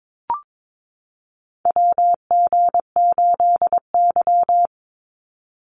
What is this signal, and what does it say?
Morse code, heard as a keyed tone, about 22 wpm, 700 Hz. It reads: WG8Y
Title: NA SOTA Chaser Callsigns 22wpm